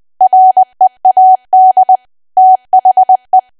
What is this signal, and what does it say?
Morse code, heard as a keyed tone, about 20 wpm, 750 Hz. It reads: READ THE
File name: Archive / 20 WPM / 20WPM